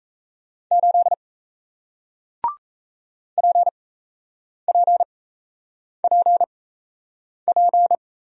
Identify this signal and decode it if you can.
Morse code, heard as a keyed tone, about 37 wpm, spaced as for 8 wpm, 700 Hz. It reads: 8 PPPP